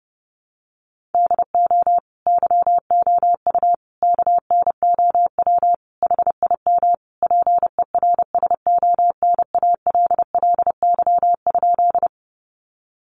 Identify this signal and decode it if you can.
Morse code, heard as a keyed tone, about 30 wpm, 700 Hz. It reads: DO YOU KNOW HIM PERSONALLY?
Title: Sentences from Top 1400 Words 30wpm